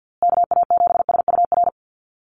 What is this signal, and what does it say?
Morse code, heard as a keyed tone, about 50 wpm, 700 Hz. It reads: KA7SUR
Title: USA Call Signs Repeated 2x 50wpm